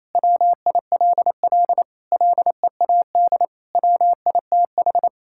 Morse code, heard as a keyed tone, about 27 wpm, 700 Hz. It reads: WILL LEAD WITH
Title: Sentences from Top 1000 Words - Review Part 5 - 28wpm